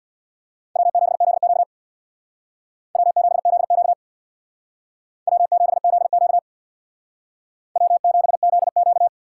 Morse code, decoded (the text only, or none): W7ZQ W7ZQ W7ZQ W7ZQ